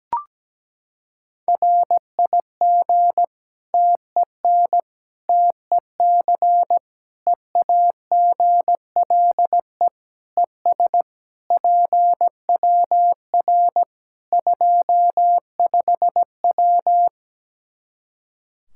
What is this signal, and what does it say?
Morse code, heard as a keyed tone, about 17 wpm, 700 Hz. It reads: RIG TEN TEC EAGLE ES PWR 25W
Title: Practice QSO - Exchange 2 - Info - 17wpm